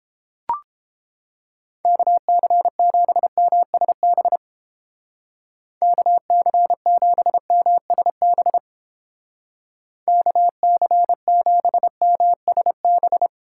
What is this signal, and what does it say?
Morse code, heard as a keyed tone, about 31 wpm, 700 Hz. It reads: KC7MSB KC7MSB KC7MSB